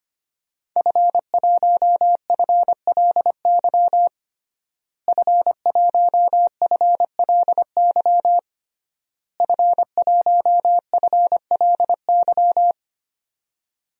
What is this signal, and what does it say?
Morse code, heard as a keyed tone, about 25 wpm, 700 Hz. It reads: F1FLY F1FLY F1FLY
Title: DXCC-100 Call Signs - 5 Characters - Repeated 3x - 25wpm